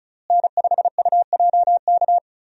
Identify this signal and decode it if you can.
Morse code, heard as a keyed tone, about 36 wpm, 700 Hz. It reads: N5UJK
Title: USA Call Signs - 5 Characters - Repeated 3x - 35wpm